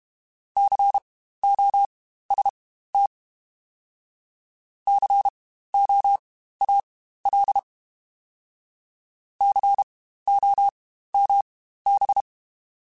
Morse code, heard as a keyed tone, about 32 wpm, spaced as for 15 wpm, 800 Hz. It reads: COST COAL COMB
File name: Spaced Alphas / Spaced Alphas mp3s / Cs sp @30